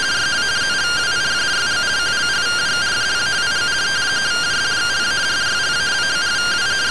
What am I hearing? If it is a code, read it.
Two-tone teleprinter signal, about 170 Hz shift.